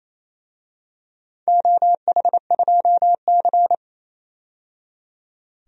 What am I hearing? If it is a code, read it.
Morse code, heard as a keyed tone, about 27 wpm, 700 Hz. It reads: OH2C
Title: DXCC-100 Call Signs - 4 Characters - 28wpm